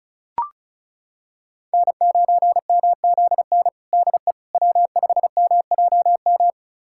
Morse code, heard as a keyed tone, about 34 wpm, 700 Hz. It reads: N9MZN DE W5MJM
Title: USA Callsign DE Callsign - 5 Characters - 35wpm